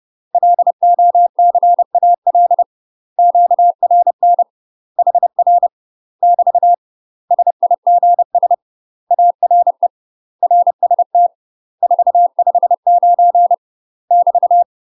Morse code, heard as a keyed tone, about 30 wpm, 700 Hz. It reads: LOCAL QRN HR <BT> SIGS ARE RST 459 <BT>
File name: UK MASTER QSOs TRAINING / 30WPM / UK MASTER QSOs 0046